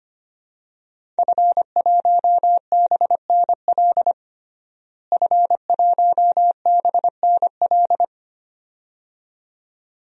Morse code, heard as a keyed tone, about 25 wpm, 700 Hz. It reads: F1BNL F1BNL
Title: DXCC-100 Call Signs - 5 Characters - Repeated 2x - 25wpm